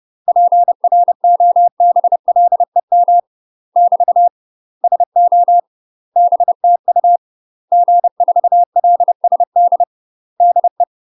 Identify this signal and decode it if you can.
Morse code, heard as a keyed tone, about 30 wpm, 700 Hz. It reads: PROBLEM <BT> SO BTU G4LSD DE
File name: UK MASTER QSOs TRAINING / 30WPM / UK MASTER QSOs 0021